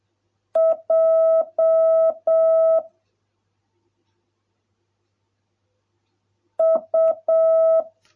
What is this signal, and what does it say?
Morse code, heard as a keyed tone, about 7 wpm, 650 Hz. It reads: J U